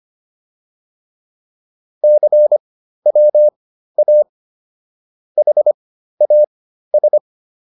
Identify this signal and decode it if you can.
Morse code, heard as a keyed tone, about 25 wpm, spaced as for 13 wpm, 600 Hz. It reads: CWA HAS